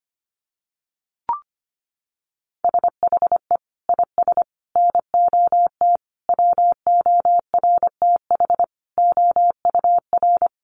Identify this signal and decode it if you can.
Morse code, heard as a keyed tone, about 25 wpm, 700 Hz. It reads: SHE IS NOT WORTH OUR
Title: Sentences from Top 1100 Words 25wpm